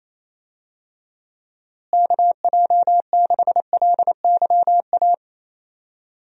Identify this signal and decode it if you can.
Morse code, heard as a keyed tone, about 28 wpm, 700 Hz. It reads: KJ6LYA
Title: USA Call Signs - 6 Characters - Repeated 2x - 28wpm